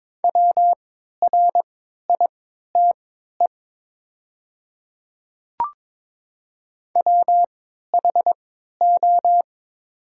Morse code, heard as a keyed tone, about 22 wpm, spaced as for 13 wpm, 690 Hz. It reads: WRITE WHO
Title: 3x Character Spacing - Top 100 Words - 22wpm